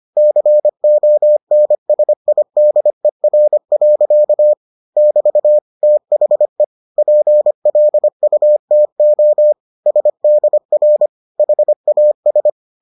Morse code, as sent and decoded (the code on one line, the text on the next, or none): -.-. --- -. ... .. -.. .-. .-.-.- / -...- / - .... / .--. .-.. ..- - --- / ... -.. .-. / .... .- ...
CONSIDER. <BT> THE PLUTO SDR HAS